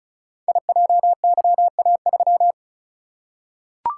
Morse code, heard as a keyed tone, about 36 wpm, 700 Hz. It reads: IJYA3
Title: Code Groups - 5 Letter:Numbers - 35wpm